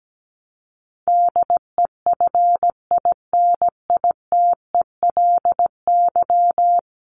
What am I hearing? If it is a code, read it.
Morse code, heard as a keyed tone, about 17 wpm, 700 Hz. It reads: DEFINITELY
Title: Top 1751-2000 Words 17wpm